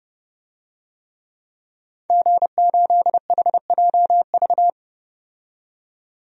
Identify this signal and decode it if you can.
Morse code, heard as a keyed tone, about 32 wpm, 700 Hz. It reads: G8HJV
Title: DXCC-100 Call Signs - 5 Characters - 30wpm